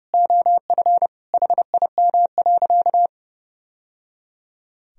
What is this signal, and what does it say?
Morse code, heard as a keyed tone, about 32 wpm, 700 Hz. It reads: OF HIM.
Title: Sentences from Top 1000 Words - Review Part 4 - 30wpm